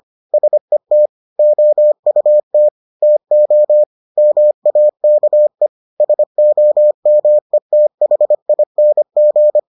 Morse code, heard as a keyed tone, about 25 wpm, 600 Hz. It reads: SET OUT TO MAKE SOMETHING